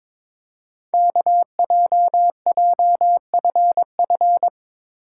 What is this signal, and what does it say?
Morse code, heard as a keyed tone, about 22 wpm, 700 Hz. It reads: KJJFF
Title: Code Groups - 5 Letters - 22wpm